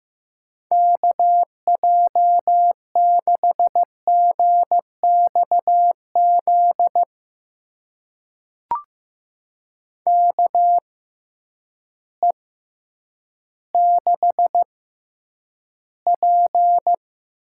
Morse code, heard as a keyed tone, about 15 wpm, spaced as for 9 wpm, 700 Hz. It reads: KJ6GXZ KE6P
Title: USA Call Signs - 6 Characters - 6x Character Spacing - 15wpm